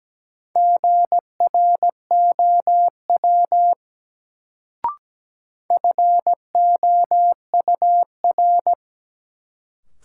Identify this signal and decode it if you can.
Morse code, heard as a keyed tone, about 17 wpm, 700 Hz. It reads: GROW FOUR
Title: Sentences from Top 300 Words - New Word Drill - 17wpm